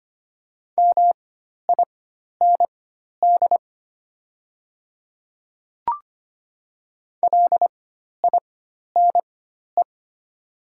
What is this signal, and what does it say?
Morse code, heard as a keyed tone, about 25 wpm, spaced as for 12 wpm, 700 Hz. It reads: MIND LINE
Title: 4x Character Spacing - Top 500 Words - 25wpm